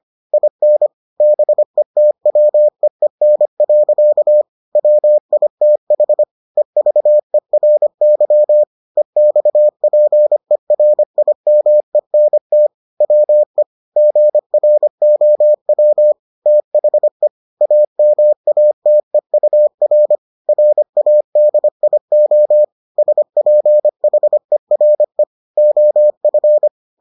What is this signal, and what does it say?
Morse code, heard as a keyed tone, about 25 wpm, 600 Hz. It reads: IN BETWEEN. WITH EVERY EXPERIMENT WE GROW THE AMATEUR RADIO SPHERE OF